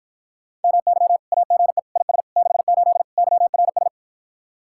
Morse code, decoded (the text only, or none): MY AGE IS 68 YRS